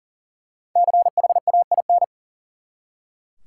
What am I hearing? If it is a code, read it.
Morse code, heard as a keyed tone, about 40 wpm, 700 Hz. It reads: CHAIN
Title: Five Letter Words - 40wpm